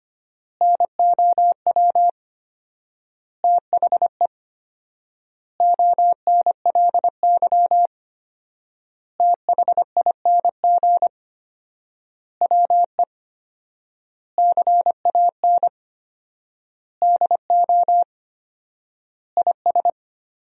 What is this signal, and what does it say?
Morse code, heard as a keyed tone, about 25 wpm, 700 Hz. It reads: NOW THE ONLY THING WE CAN DO IS
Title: 4x Word-Spacing - Sentences from Top 200 Words - 25wpm